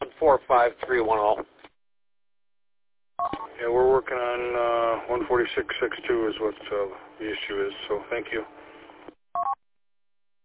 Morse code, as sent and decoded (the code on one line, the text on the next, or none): none